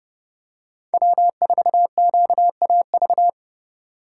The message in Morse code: W4QAV

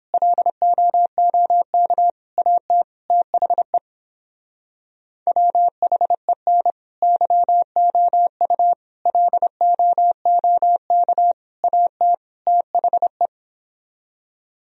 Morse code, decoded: LOOK AT THE WHEN YOU LOOK AT THE

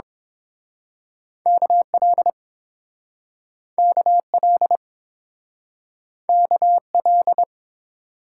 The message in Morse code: KL KL KL

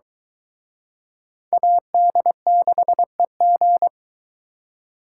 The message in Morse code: AD6EG

